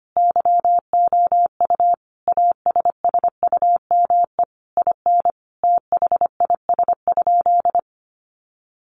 Morse code, heard as a keyed tone, about 25 wpm, 700 Hz. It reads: YOU ASSUME IN THIS?
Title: Sentences from Top 1500 Words 25wpm